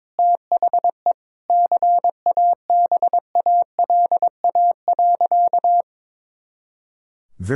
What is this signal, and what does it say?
Morse code, heard as a keyed tone, about 22 wpm, 700 Hz. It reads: THE CABALA.
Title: Pangram Challenge v2 - 22wpm